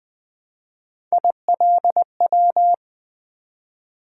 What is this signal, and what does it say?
Morse code, heard as a keyed tone, about 20 wpm, 700 Hz. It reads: ILW
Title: Three Letter-Number Combinations - 20wpm